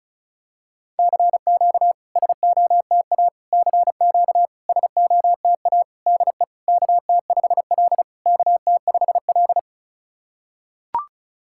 Morse code, heard as a keyed tone, about 36 wpm, 690 Hz. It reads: CQ SOTA CQ SOTA DE KT5L KT5L
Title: SOTA QSO Practice 35wpm